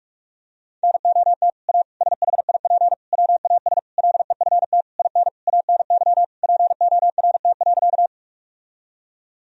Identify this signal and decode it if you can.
Morse code, heard as a keyed tone, about 46 wpm, 700 Hz. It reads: NOT A SHIP WAS LEFT IN ANY PORT.
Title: Sentences from Top 1000 Words - Review Part 2 - 45wpm